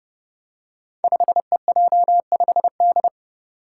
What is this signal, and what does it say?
Morse code, heard as a keyed tone, about 28 wpm, 700 Hz. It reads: 5EJ5D
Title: Five Challenging Letter-Number Combinations 30wpm